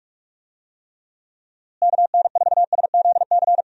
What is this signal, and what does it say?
Morse code, heard as a keyed tone, about 41 wpm, 700 Hz. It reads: KN4SZC
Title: USA Call Signs - 6 Characters - Repeated 2x - 45wpm